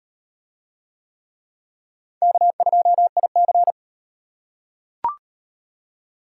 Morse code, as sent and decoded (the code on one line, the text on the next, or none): -.- ..--- .. -.-.
K2IC